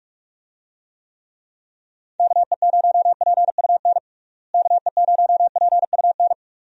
Morse code, heard as a keyed tone, about 41 wpm, 700 Hz. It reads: KE0PUN KE0PUN
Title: USA Call Signs - 6 Characters - Repeated 2x - 45wpm